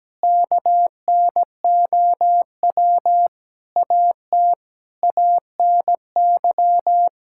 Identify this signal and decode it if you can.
Morse code, heard as a keyed tone, about 17 wpm, 700 Hz. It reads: KNOW AT ANY
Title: Sentences from Top 1000 Words - Review Part 5 - 17wpm